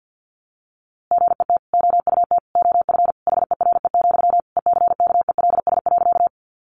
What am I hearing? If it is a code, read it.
Morse code, heard as a keyed tone, about 50 wpm, 700 Hz. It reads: GET OUT OF HERE, ECKELS.
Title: A Sound Of Thunder 50wpm